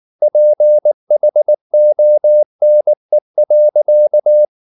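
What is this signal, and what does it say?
Morse code, heard as a keyed tone, about 19 wpm, 600 Hz. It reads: PHONE.
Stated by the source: Morse code at 19 wpm from CW med Gnister, reading Ken Follett's novel Never in English.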